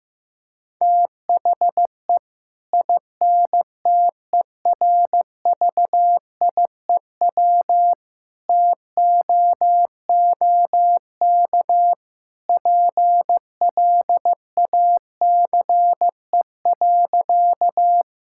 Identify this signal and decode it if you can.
Morse code, heard as a keyed tone, about 15 wpm, 700 Hz. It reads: THE INTERVIEW TOOK PLACE.
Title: Sentences from Top 1600 Words 15wpm